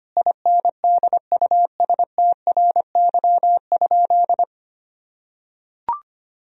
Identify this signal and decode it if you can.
Morse code, heard as a keyed tone, about 25 wpm, 700 Hz. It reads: INDUSTRY?
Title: Sentences from Top 1000 Words - Review Part 4 - 25wpm